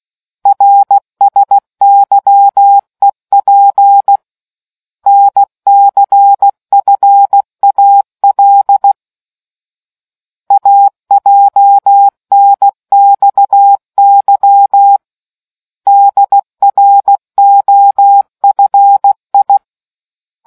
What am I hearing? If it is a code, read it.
Morse code, heard as a keyed tone, about 16 wpm, 800 Hz. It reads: RSYEP NCFAL AJNXY DROFI